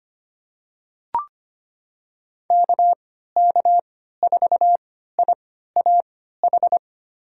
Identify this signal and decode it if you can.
Morse code, heard as a keyed tone, about 25 wpm, spaced as for 14 wpm, 700 Hz. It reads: KK4IAH
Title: USA Call Signs - 6 Characters - 3x Character Spacing - 25wpm